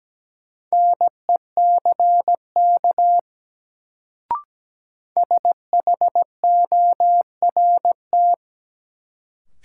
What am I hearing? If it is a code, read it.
Morse code, heard as a keyed tone, about 17 wpm, 700 Hz. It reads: NECK SHORT
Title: Sets of 5 Words Encore - New Word Drill - 17wpm